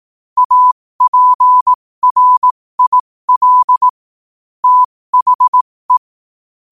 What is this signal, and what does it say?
Morse code, heard as a keyed tone, about 18 wpm, spaced as for 15 wpm, 1000 Hz. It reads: APRIL THE
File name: qotdfiles / 15wpm